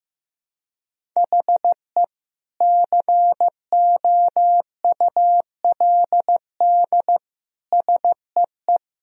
Morse code, heard as a keyed tone, about 16 wpm, 700 Hz. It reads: HE COULD SEE
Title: Sentences from Top 1000 Words - Review Part 1 - 15wpm